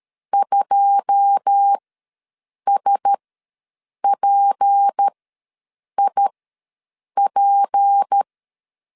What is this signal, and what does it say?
Morse code, heard as a keyed tone, about 13 wpm, spaced as for 7 wpm, 800 Hz. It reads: 2SPIP
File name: LEZIONI / 10 SIGNS / 10A SLASH ws 12wpm 3443